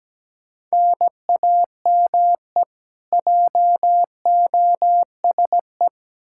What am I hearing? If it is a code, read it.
Morse code, heard as a keyed tone, about 17 wpm, 700 Hz. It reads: NAME JOSE
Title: Practice QSO - Exchange 1 - Beginning - 17wpm